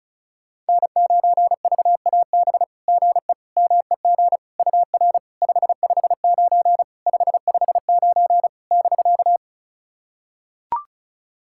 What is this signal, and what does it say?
Morse code, heard as a keyed tone, about 36 wpm, 690 Hz. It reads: N9VAB GE MEG UR 559 559 <BK>
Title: SOTA QSO Practice 35wpm